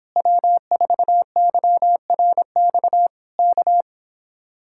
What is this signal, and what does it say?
Morse code, heard as a keyed tone, about 26 wpm, 700 Hz. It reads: W4YRX K